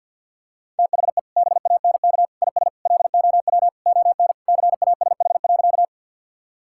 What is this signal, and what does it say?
Morse code, heard as a keyed tone, about 50 wpm, 700 Hz. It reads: THE BANK IS LOW ON CASH.
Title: Sentences from Top 1000 Words - Review Part 5 - 50wpm